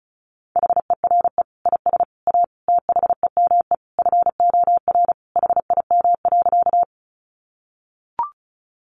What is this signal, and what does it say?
Morse code, heard as a keyed tone, about 34 wpm, 700 Hz. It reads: HERE IS A THEME FOR HIM.
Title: Sentences from Top 1100 Words 35wpm